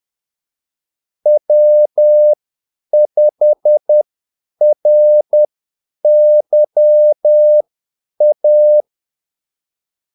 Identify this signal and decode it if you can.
Morse code, heard as a keyed tone, about 10 wpm, spaced as for 8 wpm, 600 Hz. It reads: W5RYA
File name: Callsigns-10-8wpm